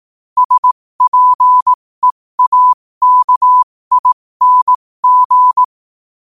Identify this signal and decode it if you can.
Morse code, heard as a keyed tone, about 18 wpm, spaced as for 15 wpm, 1000 Hz. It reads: SPEAKING
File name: qotdfiles / 15wpm